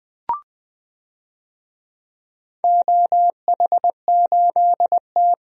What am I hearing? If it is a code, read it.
Morse code, heard as a keyed tone, about 20 wpm, 700 Hz. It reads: OH8T
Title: DXCC-100 Call Signs - 4 Characters - 20wpm